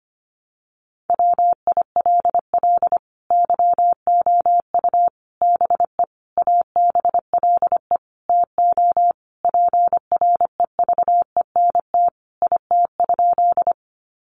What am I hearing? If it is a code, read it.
Morse code, heard as a keyed tone, about 25 wpm, 700 Hz. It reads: WILL YOU BE ABLE TO PREVENT IT?
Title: Sentences from Top 1100 Words 25wpm